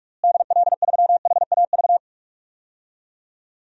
Morse code, heard as a keyed tone, about 43 wpm, 700 Hz. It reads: DL3HAV
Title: DXCC-100 Call Signs - 6 Characters - Repeated 2x - 45wpm